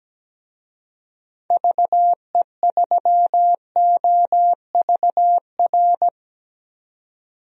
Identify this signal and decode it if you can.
Morse code, heard as a keyed tone, about 17 wpm, 700 Hz. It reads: VE3OVR
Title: DXCC-100 Call Signs - 6 Characters - 17wpm